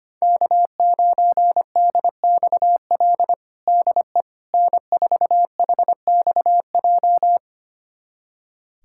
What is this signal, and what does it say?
Morse code, heard as a keyed tone, about 25 wpm, 700 Hz. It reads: K9DXL DE N4HXJ
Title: USA Callsign DE Callsign - 5 Characters - 25wpm